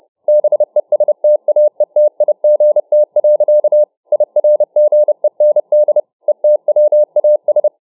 Morse code, heard as a keyed tone, about 29 wpm, 600 Hz. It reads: BESTAETIGT. IRGEND ETWAS